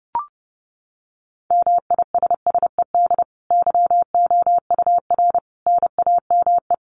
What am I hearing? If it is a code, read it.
Morse code, heard as a keyed tone, about 30 wpm, 700 Hz. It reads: MISSED YOUR NAME
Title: Common QSO Phrases 30wpm